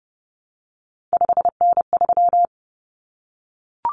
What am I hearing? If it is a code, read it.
Morse code, heard as a keyed tone, about 30 wpm, 700 Hz. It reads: 5N3
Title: Common Cut Numbers - 3-Digit 30wpm